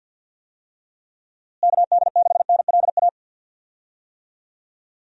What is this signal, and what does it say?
Morse code, heard as a keyed tone, about 51 wpm, 700 Hz. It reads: KD6NLA
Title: USA Call Signs - 6 Characters - Repeated 2x - 50wpm